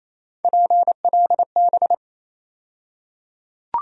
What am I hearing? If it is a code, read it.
Morse code, heard as a keyed tone, about 28 wpm, 700 Hz. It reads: PLB
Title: USA Call Sign Postfixes - 3 Letters 28wpm